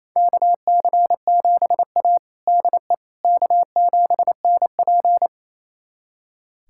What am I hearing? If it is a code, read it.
Morse code, heard as a keyed tone, about 28 wpm, 700 Hz. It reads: KC7A DE K7NP
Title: USA Callsign DE Callsign - 4 Characters - 28wpm